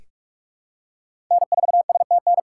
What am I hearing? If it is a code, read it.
Morse code, heard as a keyed tone, about 46 wpm, 690 Hz. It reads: N4STN
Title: USA Callsign DE Callsign 45wpm